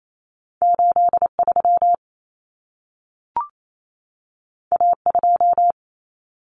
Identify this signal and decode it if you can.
Morse code, heard as a keyed tone, about 28 wpm, 700 Hz. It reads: 83 A2
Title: Common Cut Numbers - 2-Digit 28wpm